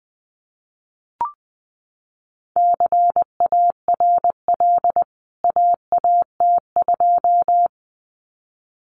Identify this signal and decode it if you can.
Morse code, heard as a keyed tone, about 20 wpm, 700 Hz. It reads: CARL AAT2
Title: CWT Exchange 20wpm